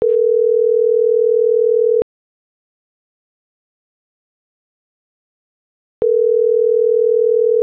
Ringback tone (North America), 440 + 480 Hz.